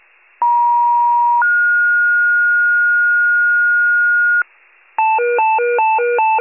Two-tone page, heard as a one-second tone, then a three-second tone, about 950 then 1430 Hz, with noise. It is heard from 0.4 to 4.4 s.
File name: wcalert